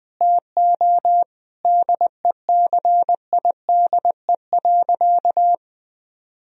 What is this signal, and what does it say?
Morse code, heard as a keyed tone, about 20 wpm, 700 Hz. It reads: TO DECIDE.
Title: Sentences from Top 1000 Words - Review Part 3 - 20wpm